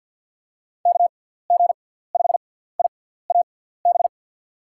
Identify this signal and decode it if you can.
Morse code, heard as a keyed tone, about 48 wpm, spaced as for 17 wpm, 700 Hz. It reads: KG5IAB
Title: USA Call Signs - 6 Characters - 6x Character Spacing - 50wpm